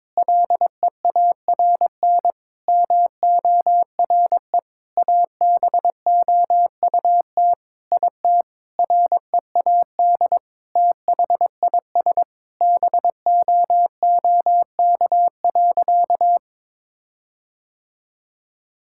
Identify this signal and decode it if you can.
Morse code, heard as a keyed tone, about 22 wpm, 700 Hz. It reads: LEARN MORE ABOUT IT READ THIS BOOK.
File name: Sentences with 7-Word Phrases - 22wpm